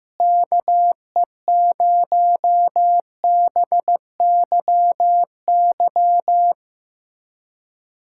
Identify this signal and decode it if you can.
Morse code, heard as a keyed tone, about 15 wpm, 700 Hz. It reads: KE0BYY
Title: USA Callsign DE Callsign - 6 Characters - 15wpm